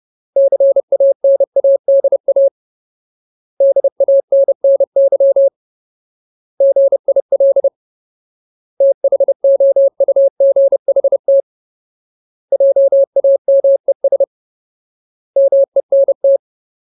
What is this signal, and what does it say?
Morse code, heard as a keyed tone, about 30 wpm, 550 Hz. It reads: CANADA DANNY GIL THOUGHT JAMES MENT
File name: Ultimate Word List / 30WPM Ultimate